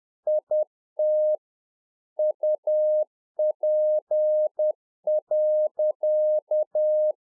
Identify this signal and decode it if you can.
Morse code, heard as a keyed tone, about 10 wpm, 610 Hz. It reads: IT UP.